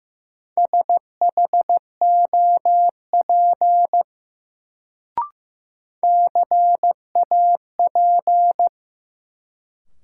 Morse code, heard as a keyed tone, about 15 wpm, 700 Hz. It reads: SHOP CAP
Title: Sentences from Top 1600 Words - New Word Drill Alt - 15wpm